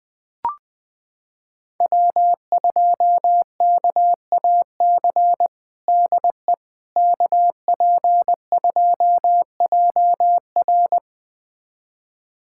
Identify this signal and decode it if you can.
Morse code, heard as a keyed tone, about 20 wpm, 700 Hz. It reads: W2KAC DE KP2JR
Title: USA Callsign DE Callsign - 5 Characters - 20wpm